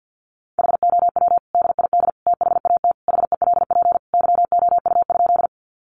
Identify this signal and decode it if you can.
Morse code, heard as a keyed tone, about 48 wpm, 700 Hz. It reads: HOW DID THAT HELP YOU?
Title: Sentences from Top 200 Words 50wpm